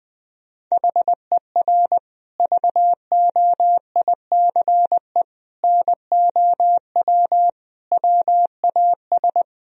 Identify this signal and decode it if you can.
Morse code, heard as a keyed tone, about 21 wpm, 700 Hz. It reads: HER VOICE NOW WAS
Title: Sentences from Top 1000 Words - Review Part 3 - 20wpm